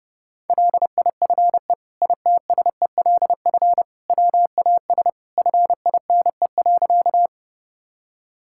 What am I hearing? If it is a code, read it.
Morse code, heard as a keyed tone, about 30 wpm, 700 Hz. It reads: LIFE ITSELF WAS FINE.
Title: Sentences from Top 1000 Words - Review Part 5 - 30wpm